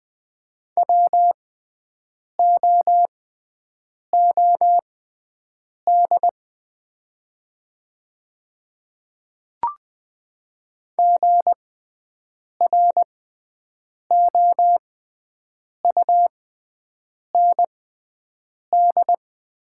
Morse code, heard as a keyed tone, about 20 wpm, spaced as for 7 wpm, 700 Hz. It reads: WOOD GROUND
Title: 6x Character Spacing - Top 500 Words - 20wpm